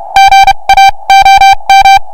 Morse code, heard as a keyed tone, about 32 wpm, spaced as for 26 wpm, 2350 Hz, with noise. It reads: GAOM